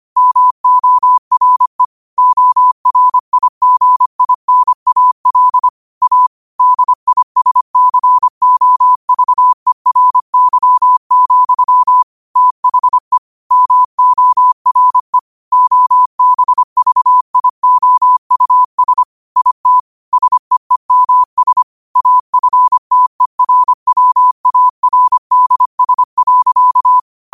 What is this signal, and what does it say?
Morse code, heard as a keyed tone, about 25 wpm, 1000 Hz. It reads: MORE ORIGINAL A DISCOVERY, THE MORE OBVIOUS IT SEEMS AFTERWARDS.